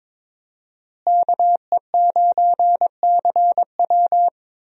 Morse code, heard as a keyed tone, about 22 wpm, 700 Hz. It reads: KE9CW
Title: USA Call Signs - 5 Characters - Repeated 3x - 22wpm